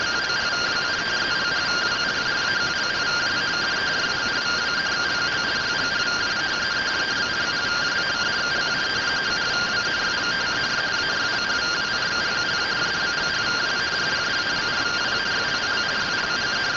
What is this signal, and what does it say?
Two-tone teleprinter signal, about 170 Hz shift.